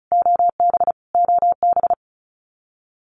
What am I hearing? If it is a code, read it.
Morse code, heard as a keyed tone, about 34 wpm, 700 Hz. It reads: OB OB
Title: Abbreviations Repeated 3x - 35wpm